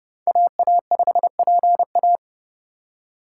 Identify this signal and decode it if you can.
Morse code, heard as a keyed tone, about 37 wpm, 700 Hz. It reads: AA5PA
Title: Speed Racing - USA 2x2 Call Signs 30wpm